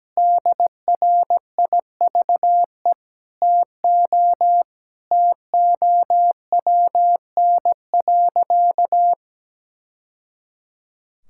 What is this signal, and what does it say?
Morse code, heard as a keyed tone, about 17 wpm, 700 Hz. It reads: DRIVE TO TOWN.
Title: Sentences from Top 1000 Words - Review Part 6 - 17wpm